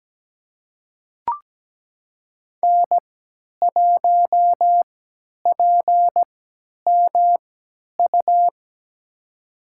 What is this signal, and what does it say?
Morse code, heard as a keyed tone, about 17 wpm, spaced as for 10 wpm, 700 Hz. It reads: N1PMU
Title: USA Call Signs - 5 Characters - 3x Character Spacing - 17wpm